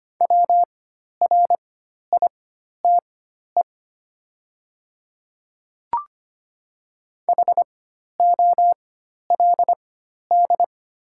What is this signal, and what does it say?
Morse code, heard as a keyed tone, about 25 wpm, spaced as for 12 wpm, 690 Hz. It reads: WRITE HOLD
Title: 4x Character Spacing - Top 500 Words - 25wpm